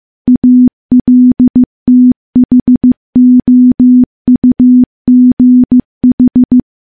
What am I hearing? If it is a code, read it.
Morse code, heard as a keyed tone, about 15 wpm, 250 Hz. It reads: ALTHOUGH
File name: morse-training-15WPM